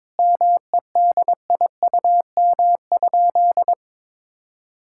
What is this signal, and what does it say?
Morse code, heard as a keyed tone, about 22 wpm, 700 Hz. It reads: MEDIUM?
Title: Sentences from Top 1000 Words - Review Part 6 - 22wpm